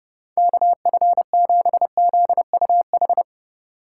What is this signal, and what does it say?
Morse code, heard as a keyed tone, about 31 wpm, 700 Hz. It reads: KF7ZUH
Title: Speed Racing - USA 2x3 Call Signs 30wpm